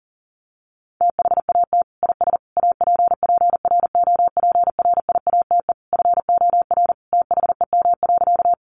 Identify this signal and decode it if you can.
Morse code, heard as a keyed tone, about 40 wpm, 700 Hz. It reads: THAT IS APPROPRIATE FOR THEM.
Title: Sentences from Top 900 Words 40wpm